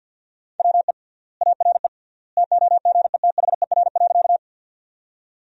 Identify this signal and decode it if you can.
Morse code, heard as a keyed tone, about 50 wpm, 700 Hz. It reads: WE ARE TOGETHER.
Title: 3x Word-Spacing - Sentences from Top 300 Words - 50wpm